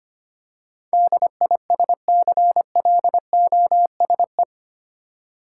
Morse code, heard as a keyed tone, about 25 wpm, 700 Hz. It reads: DISCLOSE
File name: 8 Letter Words - First 3 Letters Given - 25wpm